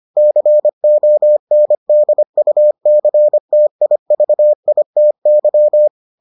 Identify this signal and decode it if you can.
Morse code, heard as a keyed tone, about 25 wpm, 600 Hz. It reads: CONDUCTIVITY